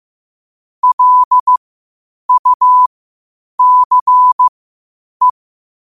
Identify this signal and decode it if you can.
Morse code, heard as a keyed tone, about 15 wpm, spaced as for 8 wpm, 1000 Hz. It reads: LUCE
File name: qotdfiles / 8wpm